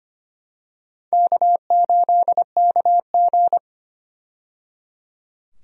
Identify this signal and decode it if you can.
Morse code, heard as a keyed tone, about 25 wpm, 700 Hz. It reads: K8KG
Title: USA Call Signs - 4 Characters - 25wpm